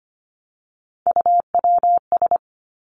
Morse code, heard as a keyed tone, about 25 wpm, 700 Hz. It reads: UWS
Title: Three Challenging Letter-Number Combinations 25wpm